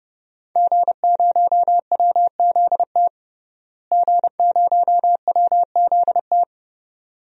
Morse code, heard as a keyed tone, about 32 wpm, 700 Hz. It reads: G0WZT G0WZT
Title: DXCC-100 Call Signs - 5 Characters - Repeated 2x - 30wpm